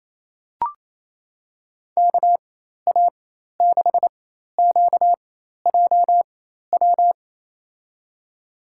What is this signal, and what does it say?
Morse code, heard as a keyed tone, about 28 wpm, spaced as for 13 wpm, 700 Hz. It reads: KA6QJW
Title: USA Call Signs - 6 Characters - 4x Character Spacing - 28wpm